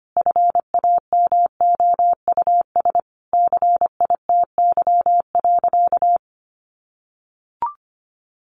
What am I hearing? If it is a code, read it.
Morse code, heard as a keyed tone, about 25 wpm, 700 Hz. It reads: FAMOUS CITY.
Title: Sentences from Top 1500 Words 25wpm